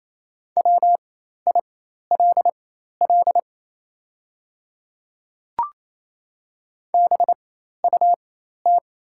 Morse code, heard as a keyed tone, about 27 wpm, spaced as for 13 wpm, 690 Hz. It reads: WILL BUT
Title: 4x Character Spacing - Top 100 Words - 28wpm